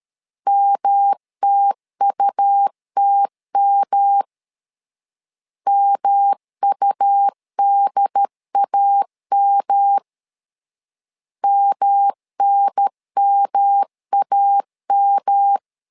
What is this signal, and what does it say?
Morse code, heard as a keyed tone, about 13 wpm, 800 Hz. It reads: MTUTM MUDAM MNMAM
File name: LEZIONI / 3 MWRO / 3 M 12wpm 1541